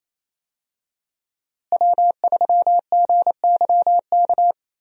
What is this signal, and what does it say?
Morse code, heard as a keyed tone, about 28 wpm, 700 Hz. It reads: W3GYK
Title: USA Call Signs - 5 Characters - Repeated 3x - 28wpm